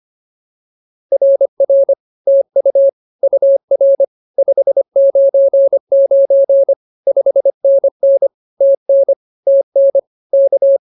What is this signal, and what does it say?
Morse code, heard as a keyed tone, about 24 wpm, 550 Hz. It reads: RR TU UR 599 5NN TN TN K